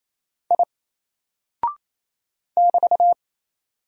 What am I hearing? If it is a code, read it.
Morse code, heard as a keyed tone, about 27 wpm, 700 Hz. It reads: I <BT>
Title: Single Character (Letter:Number:Punctuation:Prosign) 28wpm